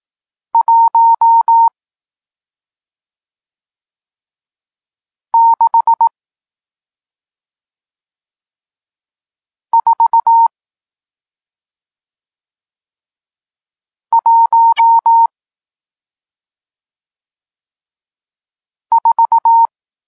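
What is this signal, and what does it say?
Morse code, heard as a keyed tone, about 18 wpm, spaced as for 2 wpm, 920 Hz. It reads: 16414
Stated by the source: Morse code lesson teaching digits 1, 4 and 6, from SP9MOA, Niepołomicki Klub Krótkofalowców (Poland) amateur radio club.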